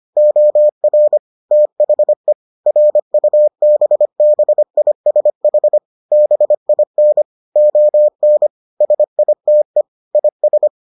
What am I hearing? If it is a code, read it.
Morse code, heard as a keyed tone, about 25 wpm, 600 Hz. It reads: OR THE RUBBISH BIN ON SITE IS